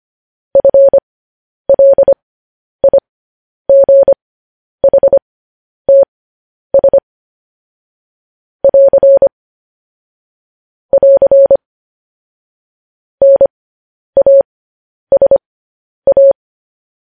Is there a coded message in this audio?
Yes: Morse code from 0.6 to 16.3 s, heard as a keyed tone, about 25 wpm, spaced as for 10 wpm, 550 Hz. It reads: FLIGHTS <AR> <AR> NASA